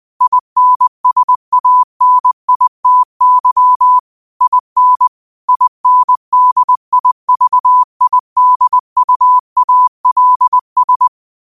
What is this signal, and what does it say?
Morse code, heard as a keyed tone, about 20 wpm, 1000 Hz. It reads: INSANITY IN INDIVIDUALS
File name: qotdfiles / 20wpm